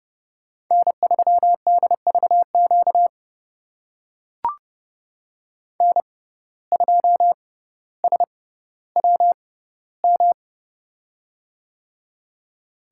Morse code, heard as a keyed tone, about 30 wpm, spaced as for 15 wpm, 700 Hz. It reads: N3DVQ N2SWM
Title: USA Call Signs - 5 Characters - 6x Character Spacing - 30wpm